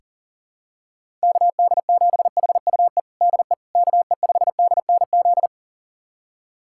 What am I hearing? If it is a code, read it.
Morse code, heard as a keyed tone, about 37 wpm, 700 Hz. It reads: KD7HUE DE KE5DNZ
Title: USA Callsign DE Callsign - 6 Characters - 40wpm